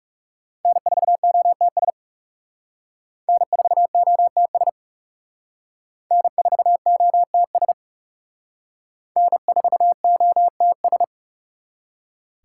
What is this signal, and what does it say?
Morse code, heard as a keyed tone, about 36 wpm, 700 Hz. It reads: N4OTS N4OTS N4OTS N4OTS
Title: Speed Racing - USA 1x3 Call Signs 30wpm